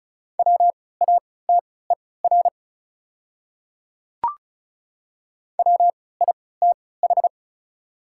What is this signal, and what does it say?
Morse code, heard as a keyed tone, about 36 wpm, spaced as for 20 wpm, 700 Hz. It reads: WATER WITH
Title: 3x Character Spacing - Top 200 Words - 35wpm